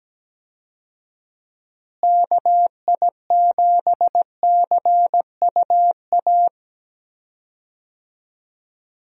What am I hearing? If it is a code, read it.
Morse code, heard as a keyed tone, about 17 wpm, 700 Hz. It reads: KI7CUA
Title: USA Call Signs - 6 Characters - Repeated 2x - 17wpm